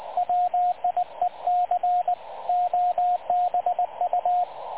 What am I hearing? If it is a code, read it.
Morse code, heard as a keyed tone, about 20 wpm, 720 Hz, with radio noise. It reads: WIEC OBU
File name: dlazaawansowanych / nagraniazpasma / websdr recording 3567.0kHz